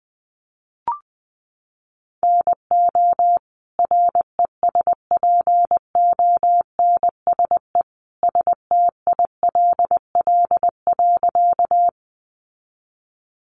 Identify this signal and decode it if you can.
Morse code, heard as a keyed tone, about 20 wpm, 700 Hz. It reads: NO RESPONSE STILL.
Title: Sentences from Top 1400 Words 20wpm